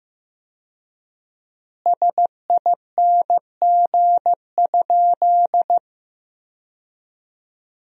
Morse code, heard as a keyed tone, about 16 wpm, 700 Hz. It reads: SING?
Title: 4x Word-Spacing - Sentences from Top 400 Words - 15wpm